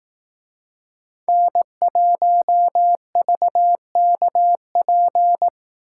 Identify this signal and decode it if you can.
Morse code, heard as a keyed tone, about 19 wpm, 700 Hz. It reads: N1VKP
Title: Speed Racing - USA 1x3 Call Signs 15wpm